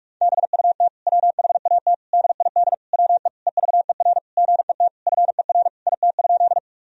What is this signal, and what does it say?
Morse code, heard as a keyed tone, about 47 wpm, 700 Hz. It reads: BUT WHAT DID WE EVER GET FER IT?
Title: Sentences from Top 400 Words and Abbreviations 45wpm